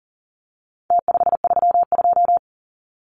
Morse code, heard as a keyed tone, about 40 wpm, 700 Hz. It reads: T532
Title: Numbers - Contest Style - 4-Digit 40wpm